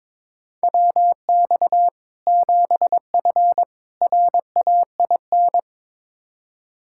Morse code, heard as a keyed tone, about 22 wpm, 700 Hz. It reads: WX 7F RAIN
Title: Practice QSO - Exchange 2 - Info - 22wpm